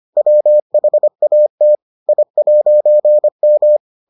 Morse code, heard as a keyed tone, about 25 wpm, 600 Hz. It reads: WHAT I'M